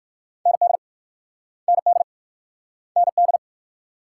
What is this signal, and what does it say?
Morse code, heard as a keyed tone, about 52 wpm, 700 Hz. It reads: ND ND ND
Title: Speed-Racing Multi-letter Phonemes - 45wpm